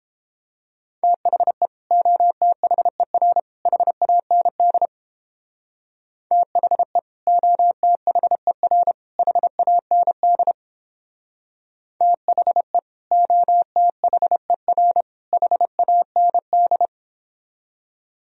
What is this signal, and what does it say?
Morse code, heard as a keyed tone, about 31 wpm, 700 Hz. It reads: THE OTHER HAND THE OTHER HAND THE OTHER HAND